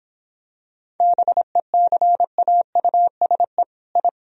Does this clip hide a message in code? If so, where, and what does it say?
Yes: Morse code from 1.0 to 4.1 s, heard as a keyed tone, about 25 wpm, 700 Hz. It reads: BECAUSE I